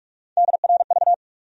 Morse code, heard as a keyed tone, about 46 wpm, 700 Hz. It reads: DRV
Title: USA Call Sign Postfixes - 3 Letters 45wpm